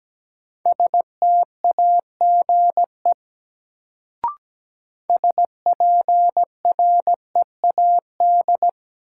Morse code, heard as a keyed tone, about 17 wpm, 700 Hz. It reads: STAGE SPREAD
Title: Sentences from Top 1200 Words - New Word Drill Alt - 17wpm